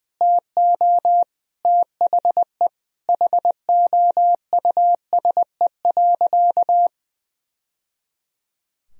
Morse code, heard as a keyed tone, about 19 wpm, 700 Hz. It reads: TO THE HOUSE.
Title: Sentences from Top 1000 Words - Review Part 3 - 20wpm